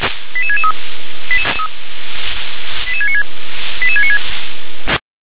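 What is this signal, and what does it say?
Five-tone selective call (ZVEI), heard as quick runs of five tones, tones of about 80 ms. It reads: 88663 88696 88696
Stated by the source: Five-tone ZVEI selective call received on an SDR.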